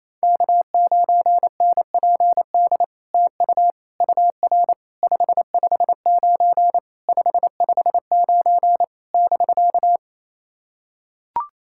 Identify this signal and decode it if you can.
Morse code, heard as a keyed tone, about 27 wpm, 660 Hz. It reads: K9NPD TU UR 559 559 <BK>
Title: SOTA QSO Practice 28wpm